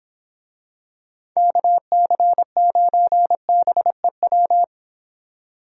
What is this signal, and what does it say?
Morse code, heard as a keyed tone, about 26 wpm, 700 Hz. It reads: KC9BEW
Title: Speed Racing - USA 2x3 Call Signs 22wpm